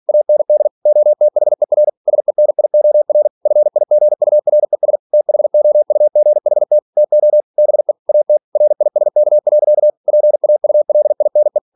Morse code, heard as a keyed tone, about 46 wpm, 600 Hz. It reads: AND OTHER SENIOR FIGURES THOUGHT TO BE AT RISK. PAULINE
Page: Never 47wpm